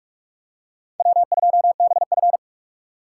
Morse code, heard as a keyed tone, about 44 wpm, 700 Hz. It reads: W2BF